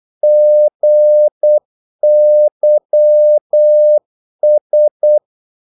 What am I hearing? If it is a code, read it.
Morse code, heard as a keyed tone, about 8 wpm, 600 Hz. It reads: GYS
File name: Korte ord DK 8wpm